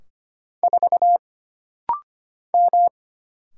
Morse code, heard as a keyed tone, about 25 wpm, 700 Hz. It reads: 4 M
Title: Single Letter-Number Alt - Rapid-Fire 25wpm